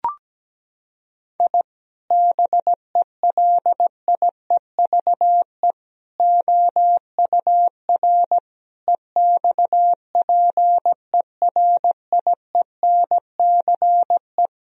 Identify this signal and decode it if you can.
Morse code, heard as a keyed tone, about 17 wpm, 700 Hz. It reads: I BELIEVE OUR EXPERIENCE
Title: Sentences from Top 1000 Words - Review Part 3 - 17wpm